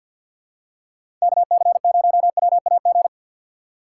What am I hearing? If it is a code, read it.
Morse code, heard as a keyed tone, about 51 wpm, 690 Hz. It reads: KC0WAG
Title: USA Call Signs - 6 Characters - 3x Character Spacing - 50wpm